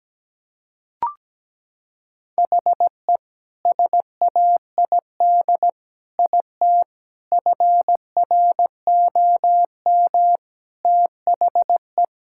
Morse code, heard as a keyed tone, about 17 wpm, 700 Hz. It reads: HE SAID IT FROM THE
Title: Sentences from Top 1600 Words 17wpm